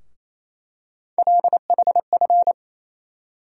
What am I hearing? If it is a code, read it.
Morse code, heard as a keyed tone, about 28 wpm, 700 Hz. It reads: LHF
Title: USA Call Sign Postfixes - 3 Letters 28wpm